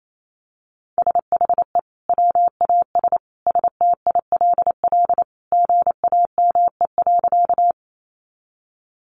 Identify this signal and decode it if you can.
Morse code, heard as a keyed tone, about 28 wpm, 700 Hz. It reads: SHE WAS STILL GAME.
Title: Sentences from Top 500 Words 28wpm